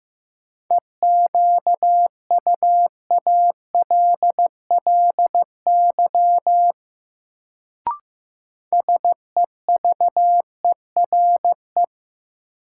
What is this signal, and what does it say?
Morse code, heard as a keyed tone, about 15 wpm, 700 Hz. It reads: EQUALLY SEVERE
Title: Sentences from Top 1400 Words - New Word Drill Alt - 15wpm